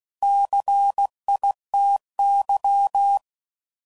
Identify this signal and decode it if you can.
Morse code, heard as a keyed tone, about 16 wpm, 800 Hz. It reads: CITY
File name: CW Code Groups @15WPM / fromSM's / Cities